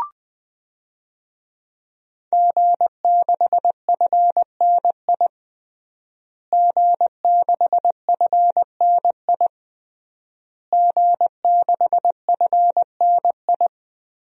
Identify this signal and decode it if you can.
Morse code, heard as a keyed tone, about 20 wpm, 700 Hz. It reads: G6FNI G6FNI G6FNI
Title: DXCC-100 Call Signs - 5 Characters - Repeated 3x - 20wpm